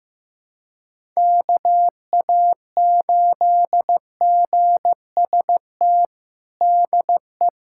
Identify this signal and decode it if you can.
Morse code, heard as a keyed tone, about 15 wpm, 700 Hz. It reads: KA8GST DE
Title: USA Callsign DE Callsign - 6 Characters - 15wpm